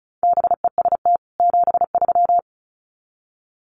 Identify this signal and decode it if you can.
Morse code, heard as a keyed tone, about 34 wpm, 700 Hz. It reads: BEST 73
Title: Common QSO Phrases 35wpm